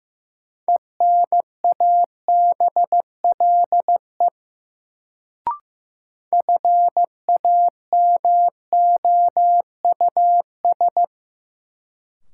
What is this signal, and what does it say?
Morse code, heard as a keyed tone, about 15 wpm, 700 Hz. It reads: ENABLE FAMOUS
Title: Sentences from Top 1500 Words - New Word Drill - 15wpm